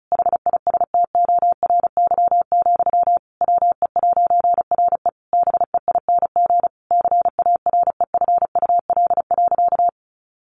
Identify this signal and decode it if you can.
Morse code, heard as a keyed tone, about 34 wpm, 700 Hz. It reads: HISTORY, WE'RE BEING CAREFUL.
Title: A Sound Of Thunder 35wpm